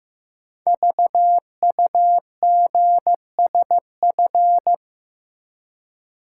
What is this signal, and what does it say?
Morse code, heard as a keyed tone, about 15 wpm, 700 Hz. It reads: VUGSF
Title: Code Groups - 5 Letters - 15wpm